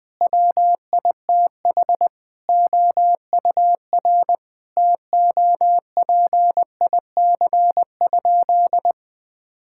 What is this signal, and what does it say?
Morse code, heard as a keyed tone, about 20 wpm, 700 Hz. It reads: WITH OUR TOPIC?
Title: Sentences with 8-Word Phrases - Spoken Beginning - 20wpm